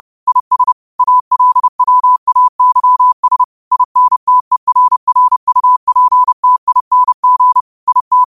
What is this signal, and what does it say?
Morse code, heard as a keyed tone, about 30 wpm, 1000 Hz. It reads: IS ALWAYS INTERRUPTING IT